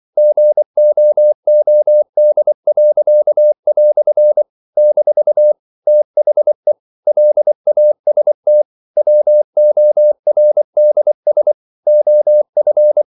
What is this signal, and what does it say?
Morse code, heard as a keyed tone, about 24 wpm, 600 Hz. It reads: GOOD." - THE LAST WORDS OF
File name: CW / Practice / EnglishQuotations0324